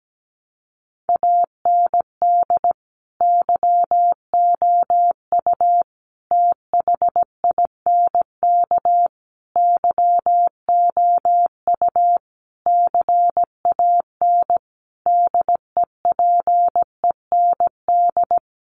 Morse code, heard as a keyed tone, about 17 wpm, 700 Hz. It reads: AND YOU THINK YOU CAN DEPEND
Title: Sentences from Top 1300 Words 17wpm